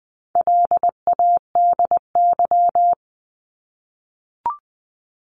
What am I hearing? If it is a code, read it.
Morse code, heard as a keyed tone, about 20 wpm, 700 Hz. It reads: LADY
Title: 260 Lesson 42e - Words 20-20wpm